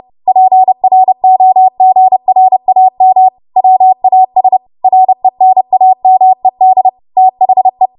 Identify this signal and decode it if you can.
Morse code, heard as a keyed tone, about 30 wpm, 750 Hz. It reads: PROGRAM WAS RENAMED THE
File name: Archive / 30 WPM / 30WPM